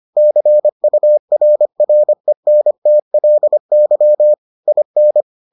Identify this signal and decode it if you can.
Morse code, heard as a keyed tone, about 25 wpm, 600 Hz. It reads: CURRENTLY IN